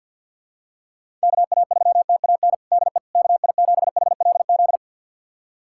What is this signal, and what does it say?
Morse code, heard as a keyed tone, about 51 wpm, 700 Hz. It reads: KA3TAN DE KI7HLZ
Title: USA Callsign DE Callsign - 6 Characters - 50wpm